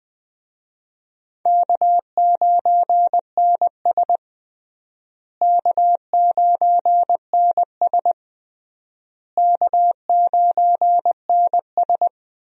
Morse code, heard as a keyed tone, about 20 wpm, 700 Hz. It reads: K9NS K9NS K9NS
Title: USA Call Signs - 4 Characters - Repeated 3x - 20wpm